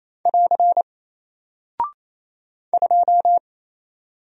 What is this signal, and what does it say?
Morse code, heard as a keyed tone, about 28 wpm, 700 Hz. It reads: <AR> 2
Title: Single Character (Letter:Number:Punctuation:Prosign) Alt 28wpm